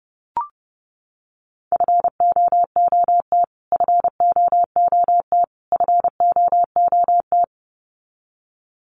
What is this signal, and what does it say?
Morse code, heard as a keyed tone, about 30 wpm, 700 Hz. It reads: FOOT FOOT FOOT
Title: Top 500 Words Repeated 30wpm